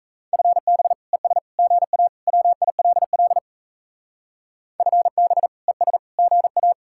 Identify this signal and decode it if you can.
Morse code, heard as a keyed tone, about 36 wpm, 700 Hz. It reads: FB ES GA WILL FB ES GA